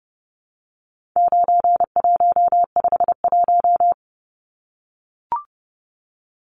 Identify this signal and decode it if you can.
Morse code, heard as a keyed tone, about 30 wpm, 700 Hz. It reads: 9151
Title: Numbers - 4-Digit 30wpm